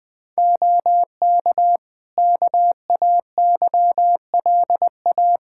Morse code, heard as a keyed tone, about 20 wpm, 700 Hz. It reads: OK KAYLA
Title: Practice QSO - Exchange 4 - Ending - 20wpm